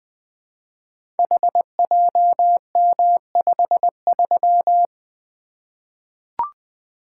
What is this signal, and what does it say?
Morse code, heard as a keyed tone, about 20 wpm, 700 Hz. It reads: HJM53
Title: Code Groups - 5 Letter:Numbers - 20wpm